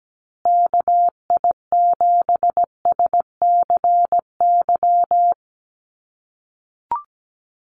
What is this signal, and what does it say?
Morse code, heard as a keyed tone, about 17 wpm, 700 Hz. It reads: KI7SCY
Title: USA Call Signs Repeated 3x 17wpm